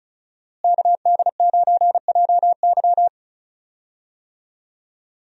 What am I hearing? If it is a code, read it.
Morse code, heard as a keyed tone, about 34 wpm, 700 Hz. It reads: KD9JY
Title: USA Call Signs - 5 Characters - Repeated 2x - 35wpm